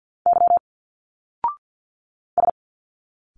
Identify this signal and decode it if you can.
Morse code, heard as a keyed tone, about 49 wpm, 700 Hz. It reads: Y S
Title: Single Letters 50wpm